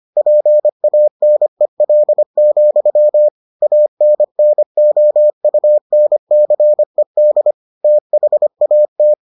Morse code, heard as a keyed tone, about 25 wpm, 600 Hz. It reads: PANEL, ANNOUNCED THAT